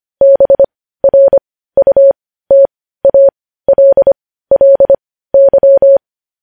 Morse code, heard as a keyed tone, about 25 wpm, spaced as for 15 wpm, 550 Hz. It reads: BRUTALLY
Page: news-headlines-in-morse-code-at-15-wpm